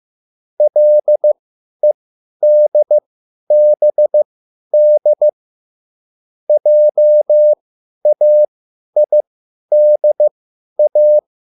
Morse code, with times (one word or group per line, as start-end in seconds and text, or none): 0.6-5.3 LEDBD
6.5-11.2 JAIDA